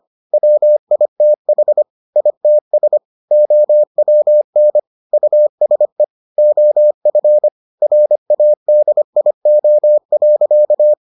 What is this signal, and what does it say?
Morse code, heard as a keyed tone, about 25 wpm, 600 Hz. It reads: WITH ITS OWN USE OF RADIO.